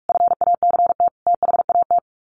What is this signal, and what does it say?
Morse code, heard as a keyed tone, about 44 wpm, 700 Hz. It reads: FACT THAT
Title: Sets of 4 Words 45wpm